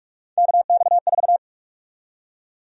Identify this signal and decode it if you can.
Morse code, heard as a keyed tone, about 46 wpm, 700 Hz. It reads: KX4
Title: USA Call Sign Prefixes - 45wpm